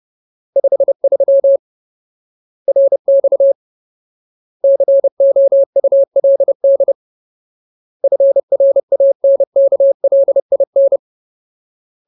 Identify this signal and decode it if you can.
Morse code, heard as a keyed tone, about 30 wpm, 550 Hz. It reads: 53 RX COULD FRANKLIN
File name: Ultimate Word List / 30WPM Ultimate